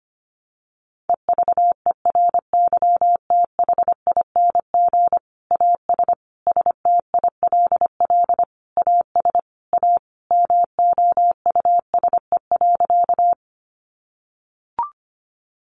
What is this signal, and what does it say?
Morse code, heard as a keyed tone, about 25 wpm, 700 Hz. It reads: EVERYTHING AS STILL AS A MOUSE.
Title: Sentences from Top 700 Words 25wpm